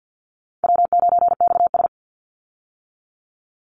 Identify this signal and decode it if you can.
Morse code, heard as a keyed tone, about 50 wpm, 700 Hz. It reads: W9XS
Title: Easy Call Signs Repeated 2x - 50wpm